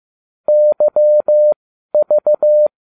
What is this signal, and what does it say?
Morse code, heard as a keyed tone, about 15 wpm, spaced as for 12 wpm, 600 Hz. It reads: YV